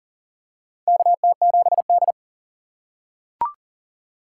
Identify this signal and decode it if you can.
Morse code, heard as a keyed tone, about 40 wpm, 700 Hz. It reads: KT7D
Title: USA Call Signs - 4x Character Spacing - 40wpm